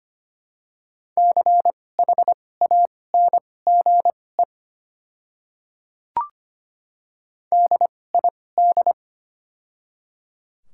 Morse code, heard as a keyed tone, about 25 wpm, spaced as for 18 wpm, 700 Hz. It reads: CHANGE DID
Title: 2x Character Spacing - Top 300 Words - 25wpm